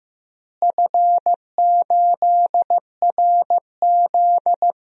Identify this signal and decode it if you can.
Morse code, heard as a keyed tone, about 15 wpm, 700 Hz. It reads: F8RZ
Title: DXCC-100 Call Signs - 4 Characters - Repeated 3x - 15wpm